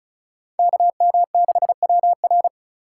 Morse code, heard as a keyed tone, about 34 wpm, 690 Hz. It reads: KM6WR
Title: USA Call Signs - 3x Character Spacing - 35wpm